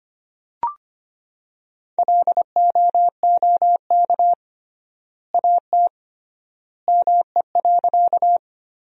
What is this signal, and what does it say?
Morse code, heard as a keyed tone, about 25 wpm, 700 Hz. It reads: LOOK AT ME.
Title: 3x Word-Spacing - Sentences from Top 200 Words - 25wpm